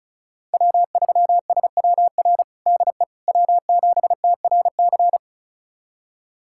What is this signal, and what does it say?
Morse code, heard as a keyed tone, about 36 wpm, 690 Hz. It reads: W3SWR DE W7TRC
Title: USA Callsign DE Callsign - 5 Characters - 35wpm